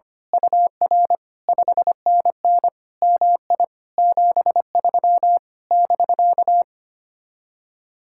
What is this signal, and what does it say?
Morse code, heard as a keyed tone, about 24 wpm, 700 Hz. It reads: UR 5NN MI 73 <BK>
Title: Speed-Racing Common QSO Phrases - 25wpm